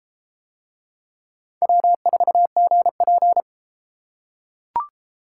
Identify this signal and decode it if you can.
Morse code, heard as a keyed tone, about 32 wpm, 700 Hz. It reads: W4GP